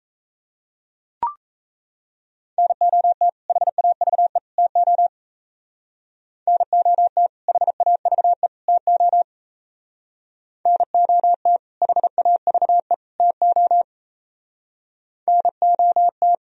Morse code, heard as a keyed tone, about 34 wpm, 700 Hz. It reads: NOT HAVE TO NOT HAVE TO NOT HAVE TO NOT
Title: Speed-Racing - Sets of 3 Words 28wpm